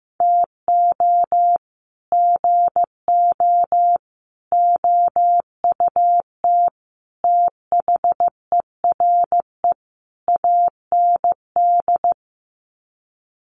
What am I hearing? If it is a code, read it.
Morse code, heard as a keyed tone, about 15 wpm, 700 Hz. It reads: TO GO OUT THERE AND
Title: Sets of 5 Words Encore2 15wpm